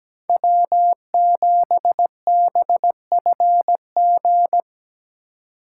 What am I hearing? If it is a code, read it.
Morse code, heard as a keyed tone, about 20 wpm, 700 Hz. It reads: W7BFG
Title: Speed Racing - USA 1x3 Call Signs 17wpm